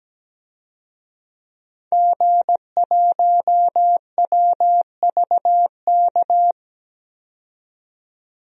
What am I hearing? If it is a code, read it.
Morse code, heard as a keyed tone, about 17 wpm, 700 Hz. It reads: G1WVK
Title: DXCC-100 Call Signs - 5 Characters - 17wpm